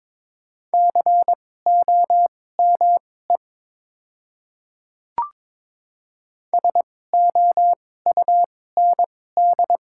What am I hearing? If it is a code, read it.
Morse code, heard as a keyed tone, about 22 wpm, spaced as for 16 wpm, 700 Hz. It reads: COME SOUND
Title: 2x Character Spacing - Top 100 Words - 22wpm